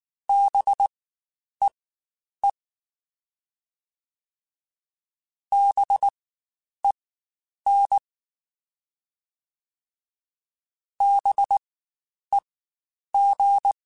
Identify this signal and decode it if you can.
Morse code, heard as a keyed tone, about 19 wpm, spaced as for 9 wpm, 800 Hz. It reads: BEE BEN BEG
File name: Spaced Alphas / Spaced Alphas mp3s / Bs sp @18